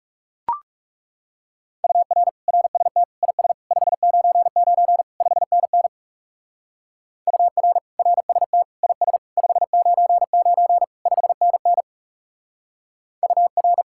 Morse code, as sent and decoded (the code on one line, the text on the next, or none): ..- .-. / .-. ... - / .. ... / ..... ----. ----. / ..... -. -. / ..- .-. / .-. ... - / .. ... / ..... ----. ----. / ..... -. -. / ..- .-.
UR RST IS 599 5NN UR RST IS 599 5NN UR